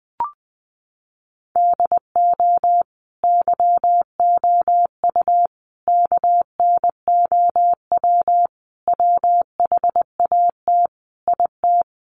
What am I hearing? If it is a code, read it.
Morse code, heard as a keyed tone, about 20 wpm, 700 Hz. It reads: DO YOU KNOW WHAT IT
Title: Sentences from Top 100 Words 20wpm